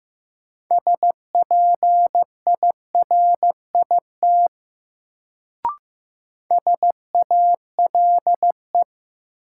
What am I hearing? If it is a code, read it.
Morse code, heard as a keyed tone, about 15 wpm, 700 Hz. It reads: SPIRIT SALE
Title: Sentences from Top 1400 Words - New Word Drill - 15wpm